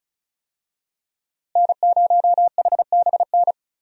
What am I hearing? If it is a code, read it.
Morse code, heard as a keyed tone, about 36 wpm, 700 Hz. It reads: N0HBN
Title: USA Call Signs - 5 Characters - Repeated 3x - 35wpm